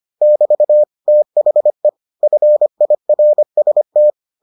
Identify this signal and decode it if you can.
Morse code, heard as a keyed tone, about 25 wpm, 600 Hz. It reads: <BT> THE FIRST